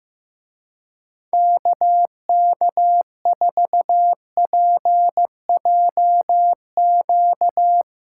Morse code, heard as a keyed tone, about 16 wpm, 700 Hz. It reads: KK4PJQ
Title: Speed Racing - USA 2x3 Call Signs 15wpm